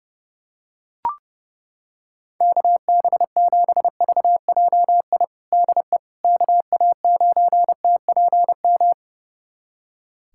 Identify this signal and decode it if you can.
Morse code, heard as a keyed tone, about 30 wpm, 700 Hz. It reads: KB7VJI DE KA9TPM
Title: USA Callsign DE Callsign - 6 Characters - 30wpm